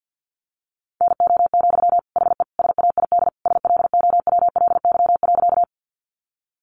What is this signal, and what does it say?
Morse code, heard as a keyed tone, about 48 wpm, 700 Hz. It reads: NO, HE SAID SLOWLY.
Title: Sentences from Top 1100 Words 50wpm